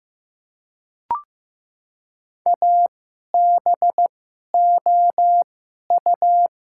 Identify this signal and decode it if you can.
Morse code, heard as a keyed tone, about 15 wpm, spaced as for 11 wpm, 700 Hz. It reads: ABOU
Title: 2x Character Spacing - Top 300 Words - 15wpm